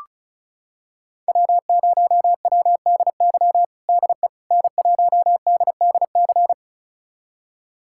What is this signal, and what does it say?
Morse code, heard as a keyed tone, about 36 wpm, 690 Hz. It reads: W0WDY DE N1DDC
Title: USA Callsign DE Callsign - 5 Characters - 35wpm